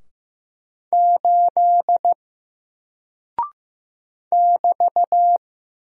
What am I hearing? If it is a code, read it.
Morse code, heard as a keyed tone, about 15 wpm, 700 Hz. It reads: 8 <BT>
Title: Single Character (Letter:Number:Punctuation:Prosign) Alt 15wpm